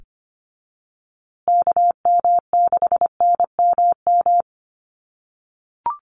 Morse code, heard as a keyed tone, about 25 wpm, 700 Hz. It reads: KM6NMM
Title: USA Call Signs Repeated 2x 25wpm